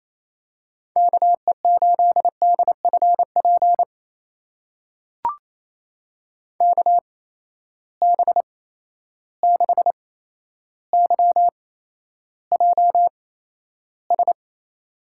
Morse code, heard as a keyed tone, about 28 wpm, spaced as for 12 wpm, 700 Hz. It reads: KE8DFP KB6YJS
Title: USA Call Signs - 6 Characters - 8x Character Spacing - 28wpm